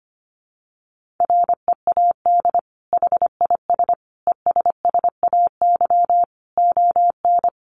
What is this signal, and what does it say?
Morse code, heard as a keyed tone, about 25 wpm, 700 Hz. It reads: READ HIS ESSAY ON
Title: Sentences from Top 1500 Words 25wpm